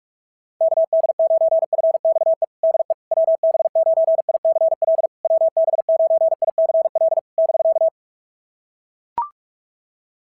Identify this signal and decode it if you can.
Morse code, heard as a keyed tone, about 44 wpm, 660 Hz. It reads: KD9FXE DE WB9ICL WB9ICL <BK>
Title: SOTA QSO Practice 45wpm